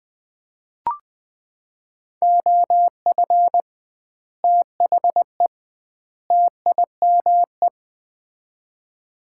Morse code, heard as a keyed tone, about 20 wpm, 700 Hz. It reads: OF THE TIME
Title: 2x Word-Spacing - Sets of 3 Words - 20wpm